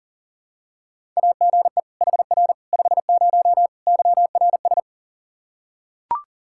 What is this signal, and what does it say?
Morse code, heard as a keyed tone, about 37 wpm, 700 Hz. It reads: AGE HR 50 YRS
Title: K1USN SST QSO Practice 40wpm